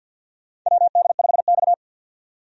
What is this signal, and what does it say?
Morse code, heard as a keyed tone, about 50 wpm, 700 Hz. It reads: WD5X
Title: USA Call Signs - 4 Characters - 50wpm